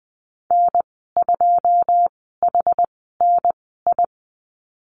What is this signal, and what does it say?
Morse code, heard as a keyed tone, about 20 wpm, spaced as for 15 wpm, 700 Hz. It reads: N2HNI
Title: 088 Lesson 15g - Number 5 - Call Signs 20wpm